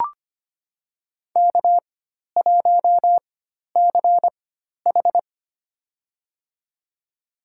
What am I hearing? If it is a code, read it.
Morse code, heard as a keyed tone, about 25 wpm, spaced as for 12 wpm, 690 Hz. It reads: K1CH
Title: USA Call Signs - 4 Characters - 4x Character Spacing - 25wpm